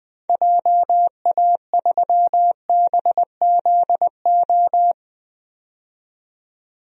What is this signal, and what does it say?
Morse code, heard as a keyed tone, about 20 wpm, 700 Hz. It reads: JA3BZO
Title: DXCC-100 Call Signs - 6 Characters - 20wpm